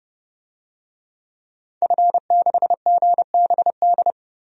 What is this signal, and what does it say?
Morse code, heard as a keyed tone, about 30 wpm, 700 Hz. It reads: F6GBD
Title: DXCC-100 Call Signs - 5 Characters - 30wpm